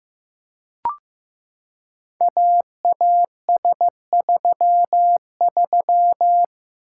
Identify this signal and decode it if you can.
Morse code, heard as a keyed tone, about 15 wpm, 700 Hz. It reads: AAS33
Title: Code Groups - 5 Letter:Numbers - 15wpm